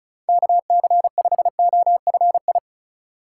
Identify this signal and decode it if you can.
Morse code, heard as a keyed tone, about 34 wpm, 700 Hz. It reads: KC5OFI
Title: USA Call Signs - 6 Characters - 2x Character Spacing - 35wpm